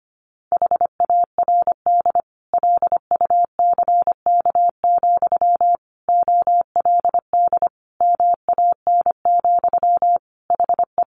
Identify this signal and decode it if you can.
Morse code, heard as a keyed tone, about 25 wpm, 700 Hz. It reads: HARD LUCK, OLD MAN, HE
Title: Sentences from Top 1500 Words 25wpm